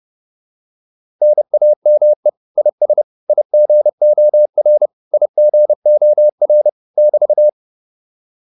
Morse code, heard as a keyed tone, about 30 wpm, 600 Hz. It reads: NAME IS IGOR IGOR <BT>